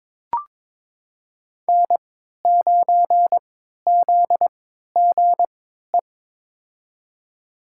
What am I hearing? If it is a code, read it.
Morse code, heard as a keyed tone, about 21 wpm, spaced as for 12 wpm, 700 Hz. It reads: N9ZGE
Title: USA Call Signs - 5 Characters - 3x Character Spacing - 22wpm